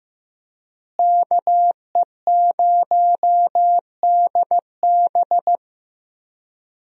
Morse code, heard as a keyed tone, about 15 wpm, 700 Hz. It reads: KE0DB
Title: USA Call Signs - 5 Characters - Repeated 3x - 15wpm